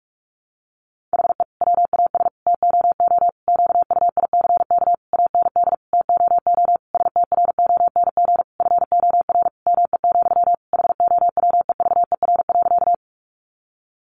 Morse code, heard as a keyed tone, about 44 wpm, 700 Hz. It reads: HE WAS TOO QUICK AND TOO STRONG FOR ME, HOWEVER.
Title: Sentences from Top 600 Words 45wpm